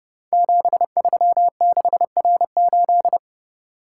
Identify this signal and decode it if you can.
Morse code, heard as a keyed tone, about 30 wpm, 700 Hz. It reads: 736R8
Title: Code Groups - 5 Letter:Numbers - 30wpm